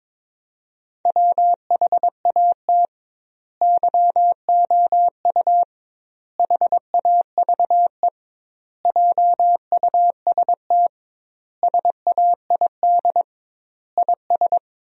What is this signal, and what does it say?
Morse code, heard as a keyed tone, about 22 wpm, 700 Hz. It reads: WHAT YOU HAVE JUST SAID IS
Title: 2x Word-Spacing - Sentences from Top 200 Words - 22wpm